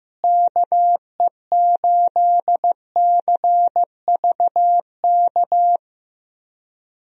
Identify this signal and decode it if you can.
Morse code, heard as a keyed tone, about 15 wpm, 700 Hz. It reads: KE8CVK
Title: USA Call Signs - 6 Characters - Repeated 2x - 15wpm